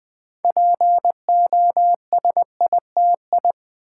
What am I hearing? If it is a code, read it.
Morse code, heard as a keyed tone, about 20 wpm, 700 Hz. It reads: POSITI
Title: Build A Word - 3 Letters At A Time - 8-Letter Words - 20wpm